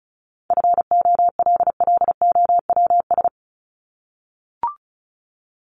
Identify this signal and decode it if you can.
Morse code, heard as a keyed tone, about 34 wpm, 700 Hz. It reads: FOLLOWS
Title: ES:S Suffix - Top 100 Words with Base Words 35wpm